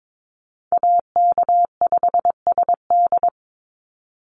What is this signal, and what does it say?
Morse code, heard as a keyed tone, about 22 wpm, 700 Hz. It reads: AK5SD
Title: NA SOTA Chaser Callsigns 22wpm